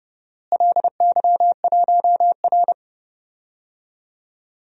Morse code, heard as a keyed tone, about 30 wpm, 700 Hz. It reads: LY1R
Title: DXCC-100 Call Signs - 4 Characters - Repeated 2x - 30wpm